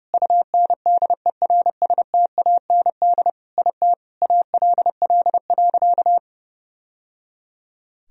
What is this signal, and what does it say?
Morse code, heard as a keyed tone, about 30 wpm, 700 Hz. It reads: UNDERSTAND IT ALL.
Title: Sentences from Top 1000 Words - Review Part 4 - 30wpm